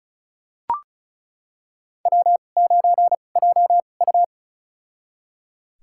Morse code, heard as a keyed tone, about 34 wpm, spaced as for 25 wpm, 700 Hz. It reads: W9JU
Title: USA Call Signs - 4 Characters - 2x Character Spacing - 35wpm